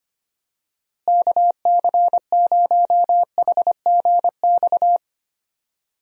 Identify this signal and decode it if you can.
Morse code, heard as a keyed tone, about 25 wpm, 700 Hz. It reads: KC0HGX
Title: USA Call Signs - 6 Characters - Repeated 3x - 25wpm